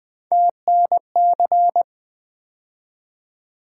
Morse code, heard as a keyed tone, about 20 wpm, 700 Hz. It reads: TNC
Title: Three Letter-Number Combinations - 20wpm